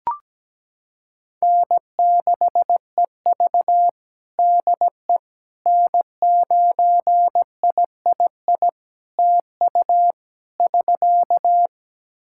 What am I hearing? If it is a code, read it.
Morse code, heard as a keyed tone, about 17 wpm, 700 Hz. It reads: N6EV DE N9III TU <SK>
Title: Practice QSO - Exchange 4 - Ending - 17wpm